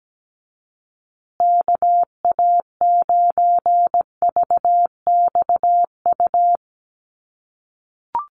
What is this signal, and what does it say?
Morse code, heard as a keyed tone, about 17 wpm, 700 Hz. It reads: KA9VXU
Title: USA Call Signs Repeated 2x 17wpm